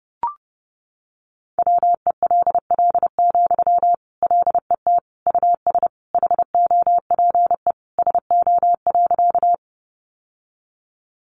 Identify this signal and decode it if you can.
Morse code, heard as a keyed tone, about 28 wpm, 700 Hz. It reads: WELL, LET US HOPE SO.
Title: Sentences from Top 1000 Words 30wpm